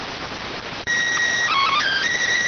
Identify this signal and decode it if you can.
SSTV (slow-scan television), PD 290.